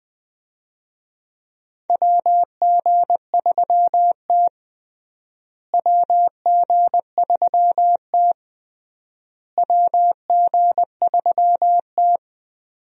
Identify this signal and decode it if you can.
Morse code, heard as a keyed tone, about 21 wpm, 700 Hz. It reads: WG3T WG3T WG3T
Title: USA Call Signs - 4 Characters - Repeated 3x - 20wpm